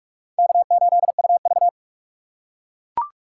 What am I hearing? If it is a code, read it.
Morse code, heard as a keyed tone, about 44 wpm, 690 Hz. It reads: K8UV
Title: USA Call Signs - 4 Characters - 3x Character Spacing - 45wpm